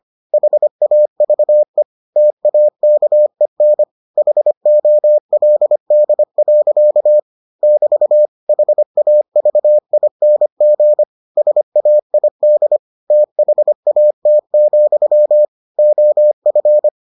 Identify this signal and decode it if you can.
Morse code, heard as a keyed tone, about 25 wpm, 600 Hz. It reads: HAVE TAKEN HOLD. <BT> HAVING SAID THAT, OF